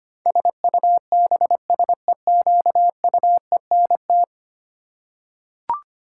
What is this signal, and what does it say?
Morse code, heard as a keyed tone, about 25 wpm, 700 Hz. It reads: SUBSEQUENT
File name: 10 Letter Words - First 3 Letters Given - 25wpm